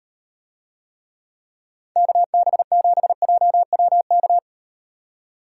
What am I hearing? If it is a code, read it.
Morse code, heard as a keyed tone, about 36 wpm, 700 Hz. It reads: KB7JWK